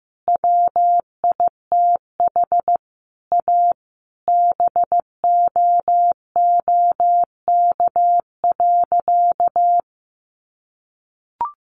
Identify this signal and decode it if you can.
Morse code, heard as a keyed tone, about 14 wpm, 700 Hz. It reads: WITH A BOOK.
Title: Easy Phrases 15wpm